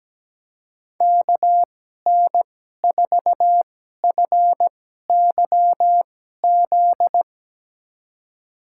Morse code, heard as a keyed tone, about 17 wpm, spaced as for 12 wpm, 700 Hz. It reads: KN4FYZ
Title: USA Call Signs - 6 Characters - 2x Character Spacing - 17wpm